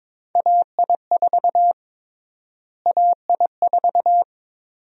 Morse code, heard as a keyed tone, about 22 wpm, 700 Hz. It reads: AI4 AI4
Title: USA Call Sign Prefixes - Repeated 2x - 22wpm